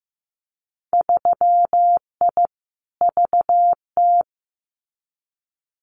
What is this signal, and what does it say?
Morse code, heard as a keyed tone, about 15 wpm, 700 Hz. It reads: 3I VT
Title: Winter Field Day Exchanges Encore 15wpm